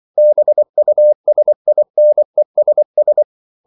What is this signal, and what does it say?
Morse code, heard as a keyed tone, about 24 wpm, 600 Hz. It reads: BUSINESS
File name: CW / Practice / 500MostCommonEnglishWords24